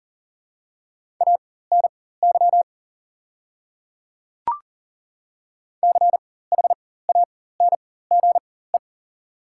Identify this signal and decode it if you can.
Morse code, heard as a keyed tone, about 40 wpm, spaced as for 19 wpm, 700 Hz. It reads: ANY CHANGE
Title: 4x Character Spacing - Top 200 Words - 40wpm